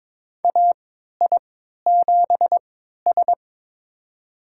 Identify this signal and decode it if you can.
Morse code, heard as a keyed tone, about 23 wpm, spaced as for 13 wpm, 700 Hz. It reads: AI7S
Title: USA Call Signs - 4 Characters - 3x Character Spacing - 22wpm